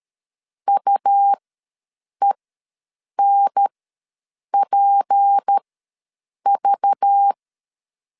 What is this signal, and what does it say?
Morse code, heard as a keyed tone, about 13 wpm, spaced as for 7 wpm, 800 Hz. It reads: UENPV
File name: LEZIONI / 6 PKX0 / 6 P ws 12wpm 3117